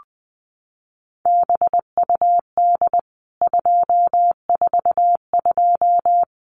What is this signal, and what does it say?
Morse code, heard as a keyed tone, about 20 wpm, 700 Hz. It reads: BUD 242
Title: CWT Exchange 20wpm